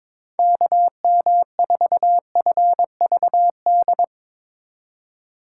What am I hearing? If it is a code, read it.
Morse code, heard as a keyed tone, about 22 wpm, 700 Hz. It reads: KM4FVD
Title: USA Call Signs - 6 Characters - 4x Character Spacing - 22wpm